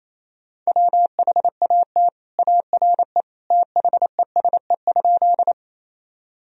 Morse code, heard as a keyed tone, about 28 wpm, 700 Hz. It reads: WHAT ARE THESE?
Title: Sentences from Top 1000 Words - Review Part 1 - 28wpm